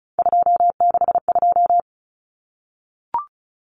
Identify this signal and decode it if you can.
Morse code, heard as a keyed tone, about 34 wpm, 700 Hz. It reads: 262
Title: Numbers - 3-Digit 35wpm